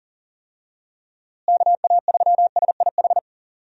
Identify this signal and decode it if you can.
Morse code, heard as a keyed tone, about 40 wpm, 700 Hz. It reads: KA3SIH
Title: USA Call Signs - 6 Characters - Repeated 3x - 40wpm